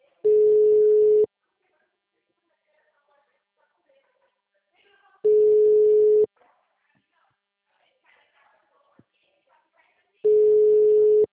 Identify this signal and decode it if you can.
Ringback tone (Europe, 425 Hz), with speech.
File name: out- -2015- - - .55181